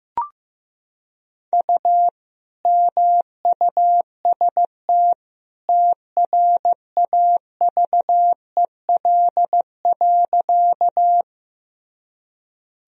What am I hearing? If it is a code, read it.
Morse code, heard as a keyed tone, about 16 wpm, 700 Hz. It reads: U MUST TRAVEL.
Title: Sentences from Top 400 Words and Abbreviations 15wpm